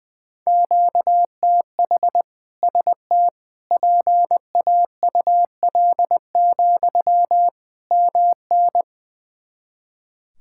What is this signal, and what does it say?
Morse code, heard as a keyed tone, about 20 wpm, 700 Hz. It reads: QTH ST PAUL, MN
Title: K1USN SST QSO Practice 20wpm